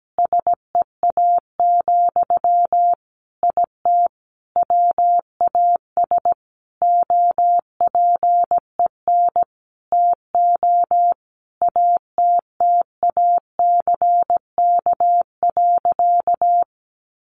Morse code, heard as a keyed tone, about 17 wpm, 700 Hz. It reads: SEA, IT WAS OPEN TO ATTACK.
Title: Sentences from Top 1500 Words 17wpm